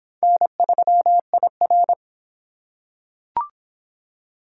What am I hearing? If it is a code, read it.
Morse code, heard as a keyed tone, about 22 wpm, 700 Hz. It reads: N3IR